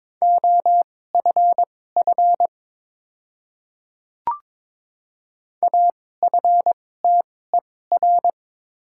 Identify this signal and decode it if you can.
Morse code, heard as a keyed tone, about 22 wpm, spaced as for 16 wpm, 700 Hz. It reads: OFF AFTER
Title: 2x Character Spacing - Top 500 Words - 22wpm